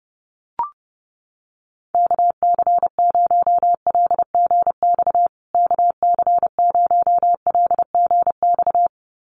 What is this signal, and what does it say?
Morse code, heard as a keyed tone, about 30 wpm, 700 Hz. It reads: KC0LGX KC0LGX
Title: USA Call Signs Repeated 2x 30wpm